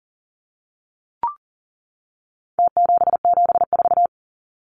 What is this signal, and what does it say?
Morse code, heard as a keyed tone, about 40 wpm, 700 Hz. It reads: T774
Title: Numbers - Contest Style - 4-Digit 40wpm